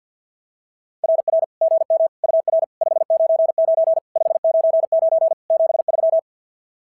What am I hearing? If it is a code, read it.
Morse code, heard as a keyed tone, about 50 wpm, 690 Hz. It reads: RR GM UR 599 599 73
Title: SOTA QSO Practice 50wpm